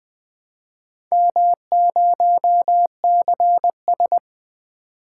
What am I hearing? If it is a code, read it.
Morse code, heard as a keyed tone, about 20 wpm, 700 Hz. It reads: M0CS
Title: Four Letter-Number Combinations - 20wpm